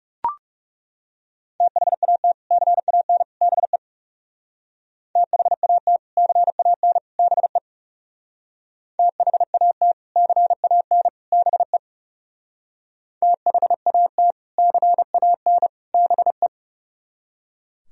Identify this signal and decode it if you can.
Morse code, heard as a keyed tone, about 37 wpm, 700 Hz. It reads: THAT CAN BE THAT CAN BE THAT CAN BE THAT CAN BE
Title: Speed-Racing - Sets of 3 Words 30wpm